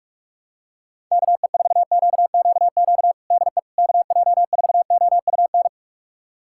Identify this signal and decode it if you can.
Morse code, heard as a keyed tone, about 41 wpm, 700 Hz. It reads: KE4QQQ DE KJ4OUN
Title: USA Callsign DE Callsign - 6 Characters - 45wpm